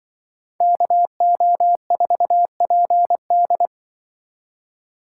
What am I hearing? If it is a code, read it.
Morse code, heard as a keyed tone, about 25 wpm, 700 Hz. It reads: KO4PD
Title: Speed Racing - USA 2x2 Call Signs 20wpm